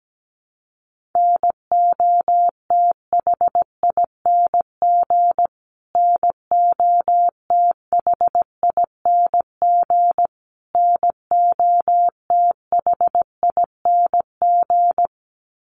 Morse code, heard as a keyed tone, about 17 wpm, 700 Hz. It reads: NOTHING NOTHING NOTHING
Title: Top 500 Words Repeated 17wpm